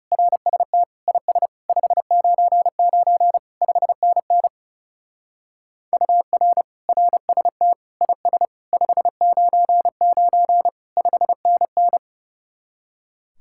Morse code, heard as a keyed tone, about 35 wpm, 700 Hz. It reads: RST IS 599 5NN UR RST IS 599 5NN
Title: Speed-Racing Common QSO Phrases - 30wpm